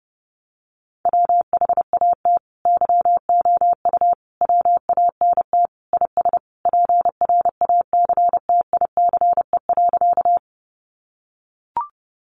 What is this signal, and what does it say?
Morse code, heard as a keyed tone, about 30 wpm, 700 Hz. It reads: WHAT YOU WANT IS PRACTICE.
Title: Sentences from Top 700 Words 30wpm